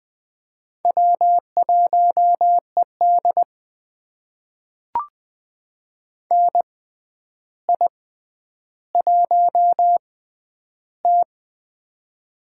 Morse code, heard as a keyed tone, about 20 wpm, spaced as for 10 wpm, 700 Hz. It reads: W1ED NI1T
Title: USA Call Signs - 4 Characters - 6x Character Spacing - 20wpm